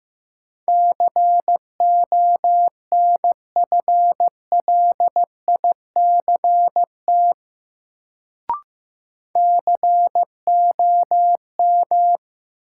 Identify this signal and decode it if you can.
Morse code, heard as a keyed tone, about 15 wpm, 700 Hz. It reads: CONFLICT COM
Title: Sentences from Top 1300 Words - New Word Drill Alt - 15wpm